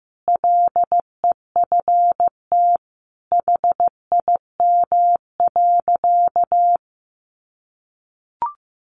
Morse code, heard as a keyed tone, about 15 wpm, 700 Hz. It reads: LEFT HIM.
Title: Sentences from Top 1500 Words 15wpm